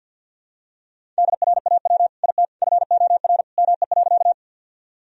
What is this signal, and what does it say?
Morse code, heard as a keyed tone, about 49 wpm, 700 Hz. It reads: DRAW IT FOR ME.
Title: Sentences from Top 1000 Words - Review Part 1 - 50wpm